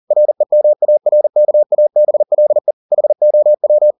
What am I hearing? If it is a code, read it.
Morse code, heard as a keyed tone, about 40 wpm, 600 Hz. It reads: REMARKABLE HOW